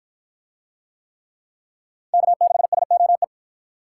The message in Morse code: K6SOE

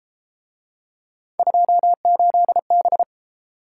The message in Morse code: 28B

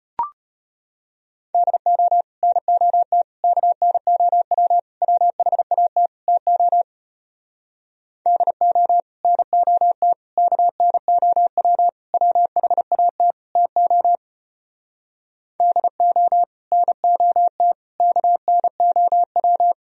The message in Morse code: DO NOT KNOW WHAT TO DO NOT KNOW WHAT TO DO NOT KNOW

